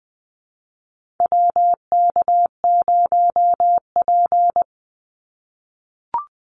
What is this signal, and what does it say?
Morse code, heard as a keyed tone, about 19 wpm, 700 Hz. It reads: WK0P
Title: Call Signs - 20wpm